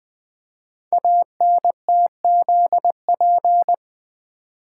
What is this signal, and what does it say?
Morse code, heard as a keyed tone, about 20 wpm, 700 Hz. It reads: ANTZP
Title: Code Groups - 5 Letters - 20wpm